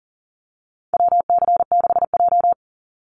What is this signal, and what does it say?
Morse code, heard as a keyed tone, about 40 wpm, 700 Hz. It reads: WC6J
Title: NA SOTA Activator Callsigns 40wpm